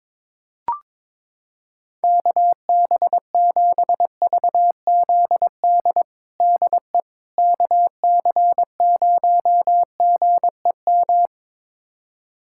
Morse code, heard as a keyed tone, about 22 wpm, 700 Hz. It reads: KB7VZD DE KC0GEM
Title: USA Callsign DE Callsign 22wpm